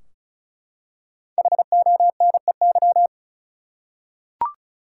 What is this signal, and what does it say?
Morse code, heard as a keyed tone, about 36 wpm, 700 Hz. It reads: HONEY
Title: Five Letter Words - 35wpm